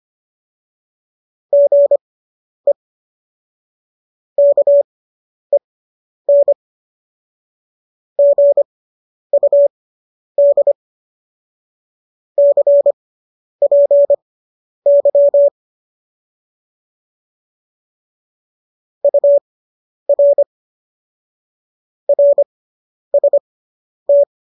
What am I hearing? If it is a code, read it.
Morse code, heard as a keyed tone, about 25 wpm, spaced as for 10 wpm, 570 Hz. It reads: GE KEN GUD CPY UR RST